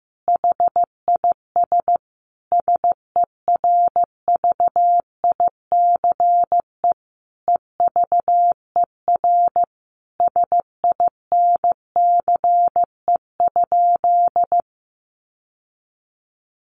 Morse code, heard as a keyed tone, about 14 wpm, 700 Hz. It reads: HIS SERVICE EVER SINCE?
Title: Sentences from Top 700 Words 15wpm